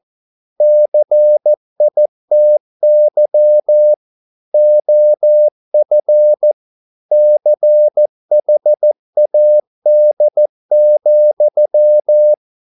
Morse code, heard as a keyed tone, about 14 wpm, 600 Hz. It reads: CITY OF CHAD,